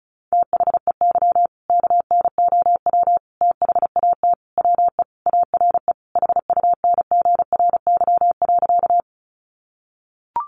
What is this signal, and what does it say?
Morse code, heard as a keyed tone, about 34 wpm, 700 Hz. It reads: THEY KNOW THAT WE ARE HUNGRY.
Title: Sentences from Top 1400 Words 35wpm